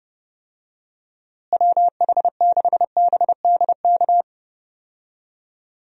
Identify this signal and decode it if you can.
Morse code, heard as a keyed tone, about 32 wpm, 700 Hz. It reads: WH6BDK
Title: USA Call Signs - 6 Characters - Repeated 2x - 30wpm